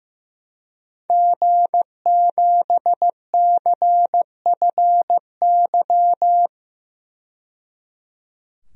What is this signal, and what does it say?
Morse code, heard as a keyed tone, about 15 wpm, 700 Hz. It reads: G7CFY
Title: DXCC-100 Call Signs - 5 Characters - 15wpm